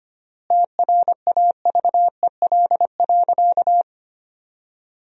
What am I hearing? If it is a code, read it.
Morse code, heard as a keyed tone, about 25 wpm, 700 Hz. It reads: TRAVEL.
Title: 3x Word-Spacing - Sentences from Top 400 Words - 25wpm